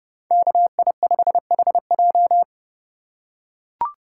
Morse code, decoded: KI5HJ

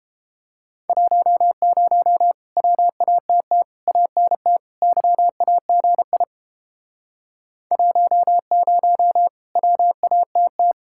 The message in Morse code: 10 WATT ANT YAGI 10 WATT

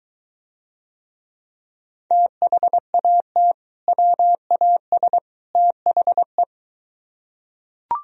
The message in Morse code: THAT WAS THE

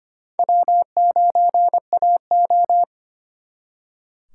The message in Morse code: W9AO

